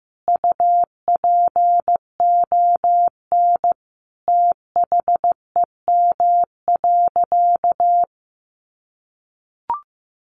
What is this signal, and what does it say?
Morse code, heard as a keyed tone, about 15 wpm, 700 Hz. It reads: UPON THEM.
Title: Sentences from Top 1400 Words 15wpm